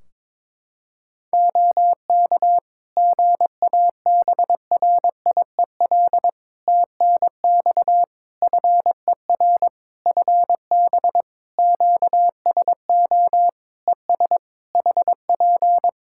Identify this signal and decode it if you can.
Morse code, heard as a keyed tone, about 21 wpm, 700 Hz. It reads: OK GABRIEL TNX FER FB QSO ES HP
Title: Practice QSO - Exchange 4 - Ending - 22wpm